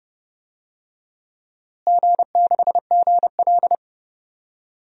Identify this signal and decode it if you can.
Morse code, heard as a keyed tone, about 30 wpm, 700 Hz. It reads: G6GL
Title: DXCC-100 Call Signs - 4 Characters - Repeated 2x - 30wpm